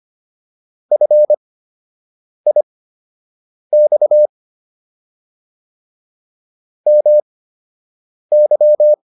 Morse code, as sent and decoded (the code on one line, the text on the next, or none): ..-. .. -..- / -- -.--
FIX MY